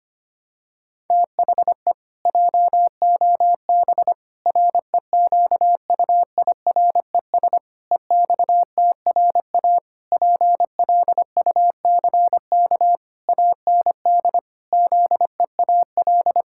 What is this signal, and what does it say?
Morse code, heard as a keyed tone, about 25 wpm, 700 Hz. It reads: THE JOB REQUIRES EXTRA PLUCK AND ZEAL
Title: Pangram Challenge v2 - 25wpm